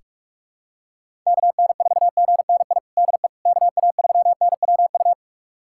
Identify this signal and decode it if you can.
Morse code, heard as a keyed tone, about 41 wpm, 700 Hz. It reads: KN4GNI DE KA3NWU
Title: USA Callsign DE Callsign - 6 Characters - 45wpm